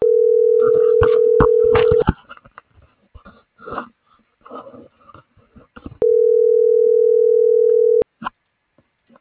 Ringback tone (North America), 440 + 480 Hz.